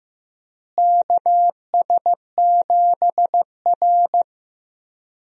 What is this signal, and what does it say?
Morse code, heard as a keyed tone, about 15 wpm, 700 Hz. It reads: KS7R
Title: USA Call Signs - 4 Characters - Repeated 3x - 15wpm